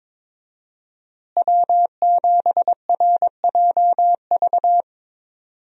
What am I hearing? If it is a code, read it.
Morse code, heard as a keyed tone, about 22 wpm, 700 Hz. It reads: W7RJV
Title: USA Call Signs - 5 Characters - Repeated 3x - 22wpm